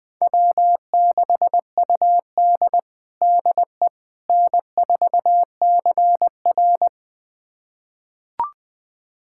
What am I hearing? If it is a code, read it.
Morse code, heard as a keyed tone, about 20 wpm, 700 Hz. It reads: W6UD DE N4CR
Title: USA Callsign DE Callsign - 4 Characters - 20wpm